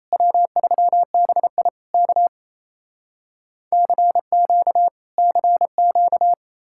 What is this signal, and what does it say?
Morse code, heard as a keyed tone, about 37 wpm, 700 Hz. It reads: W3BI K CQ CQ